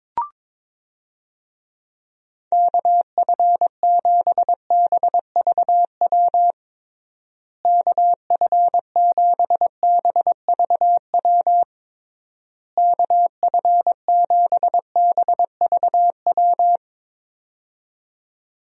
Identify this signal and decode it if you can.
Morse code, heard as a keyed tone, about 22 wpm, 700 Hz. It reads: KF7BVW KF7BVW KF7BVW
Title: USA Call Signs - 6 Characters - Repeated 3x - 22wpm